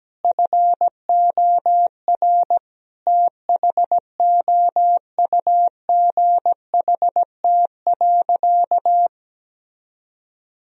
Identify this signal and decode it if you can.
Morse code, heard as a keyed tone, about 17 wpm, 700 Hz. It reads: FOR THOUGHT.
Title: Sentences from Top 1000 Words - Review Part 4 - 17wpm